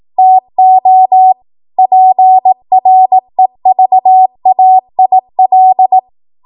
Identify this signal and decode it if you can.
Morse code, heard as a keyed tone, about 18 wpm, 750 Hz. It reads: TO PREVAIL